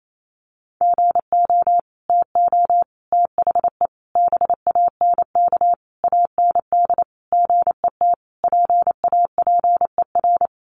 Morse code, heard as a keyed tone, about 28 wpm, 700 Hz. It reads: GO TO THE BANK AND GET PAPER
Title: Sentences from Top 1500 Words 28wpm